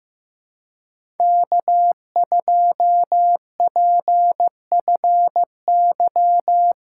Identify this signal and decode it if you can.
Morse code, heard as a keyed tone, about 15 wpm, 700 Hz. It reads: K2PFY
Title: USA Callsign DE Callsign 15wpm